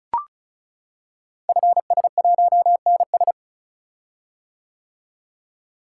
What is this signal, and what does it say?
Morse code, heard as a keyed tone, about 36 wpm, 690 Hz. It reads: FS1NS
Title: Code Groups - 5 Letter:Numbers - 35wpm